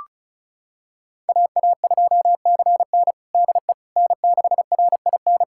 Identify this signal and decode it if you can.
Morse code, heard as a keyed tone, about 34 wpm, 700 Hz. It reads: AA2CN DE N6RIN
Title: USA Callsign DE Callsign - 5 Characters - 35wpm